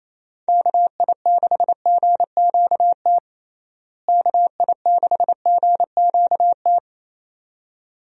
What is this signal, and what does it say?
Morse code, heard as a keyed tone, about 28 wpm, 700 Hz. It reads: KI6GQT KI6GQT
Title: USA Call Signs - 6 Characters - Repeated 2x - 28wpm